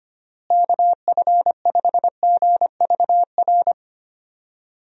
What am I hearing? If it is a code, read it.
Morse code, heard as a keyed tone, about 25 wpm, 700 Hz. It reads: KF5GVR
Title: USA Call Signs - 6 Characters - Repeated 3x - 25wpm